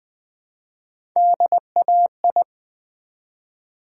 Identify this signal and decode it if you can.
Morse code, heard as a keyed tone, about 20 wpm, 700 Hz. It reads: DAI
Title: Build A Word - 1 Letter At A Time - 3 to 5-Letter Words - 20wpm